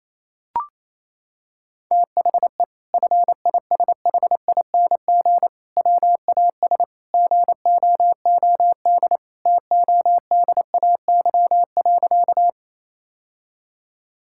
Morse code, heard as a keyed tone, about 27 wpm, 700 Hz. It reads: THE FISHING WAS GOOD TODAY.
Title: Sentences from Top 1000 Words - Review Part 5 - 28wpm